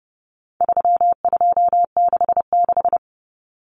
Morse code, heard as a keyed tone, about 29 wpm, 700 Hz. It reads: 3266
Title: Numbers - 4-Digit 30wpm